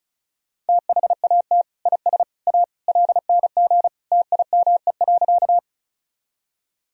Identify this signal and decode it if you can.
Morse code, heard as a keyed tone, about 36 wpm, 700 Hz. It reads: THAT IS A LNG TIME.
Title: Sentences from Top 100 Words and Abbreviations 35wpm